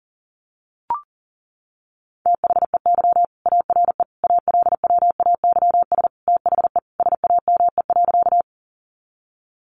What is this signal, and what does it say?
Morse code, heard as a keyed tone, about 40 wpm, 700 Hz. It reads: THEY ARE ALWAYS THE SAME.
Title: Sentences from Top 300 Words 40wpm